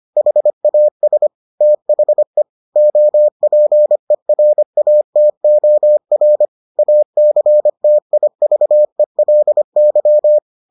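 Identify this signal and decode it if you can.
Morse code, heard as a keyed tone, about 25 wpm, 600 Hz. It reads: HAS THE OPERATOR ACTIVELY
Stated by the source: Morse edition of the Foundations of Amateur Radio episodes.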